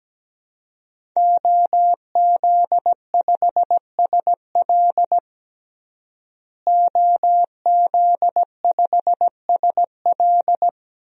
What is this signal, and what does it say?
Morse code, heard as a keyed tone, about 17 wpm, 700 Hz. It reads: OZ5SL OZ5SL
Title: DXCC-100 Call Signs - 5 Characters - Repeated 2x - 17wpm